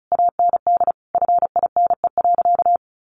Morse code, heard as a keyed tone, about 34 wpm, 700 Hz. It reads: AND FINE.
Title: Sentences from Top 1200 Words 35wpm